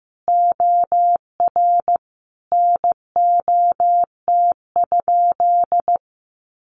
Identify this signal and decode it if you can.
Morse code, heard as a keyed tone, about 15 wpm, 700 Hz. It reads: OR NOT?
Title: Sentences from Top 1300 Words 15wpm